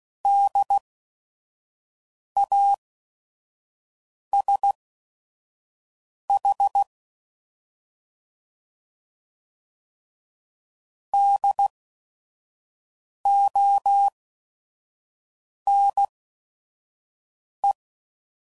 Morse code, heard as a keyed tone, about 16 wpm, spaced as for 5 wpm, 800 Hz. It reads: DASH DONE
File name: Prefixes / By First Letter / Ds